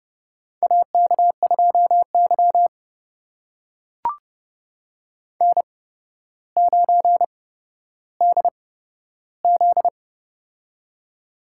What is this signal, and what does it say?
Morse code, heard as a keyed tone, about 30 wpm, spaced as for 13 wpm, 700 Hz. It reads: AK2Y N9DZ
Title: USA Call Signs - 4 Characters - 8x Character Spacing - 30wpm